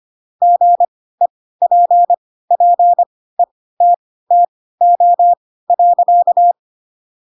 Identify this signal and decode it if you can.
Morse code, heard as a keyed tone, about 25 wpm, spaced as for 16 wpm, 700 Hz. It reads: GEPPETTO.
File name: Pinocchio / Pinocchio ch3 10 25 ik1hge